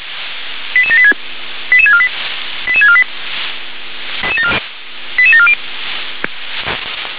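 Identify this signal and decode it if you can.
Five-tone selective call (ZVEI), heard as quick runs of five tones, tones of about 70 ms. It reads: 88786 88648 88648 88740
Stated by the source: ZVEI five-tone call 88786 received on an SDR.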